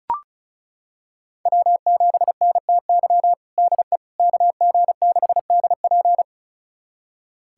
Morse code, heard as a keyed tone, about 36 wpm, 700 Hz. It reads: W7NTY DE KG6DP
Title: USA Callsign DE Callsign - 5 Characters - 35wpm